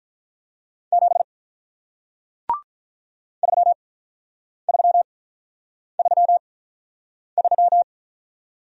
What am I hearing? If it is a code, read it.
Morse code, heard as a keyed tone, about 47 wpm, spaced as for 9 wpm, 700 Hz. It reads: 7 3333